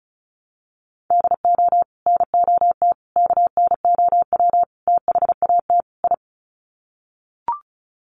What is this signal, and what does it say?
Morse code, heard as a keyed tone, about 34 wpm, 700 Hz. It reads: DO NOT KNOW THAT I
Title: Sets of 5 Words Encore2 35wpm